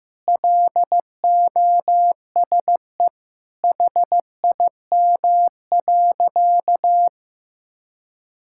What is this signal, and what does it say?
Morse code, heard as a keyed tone, about 15 wpm, 700 Hz. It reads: LOSE HIM.
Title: Sentences from Top 1600 Words 15wpm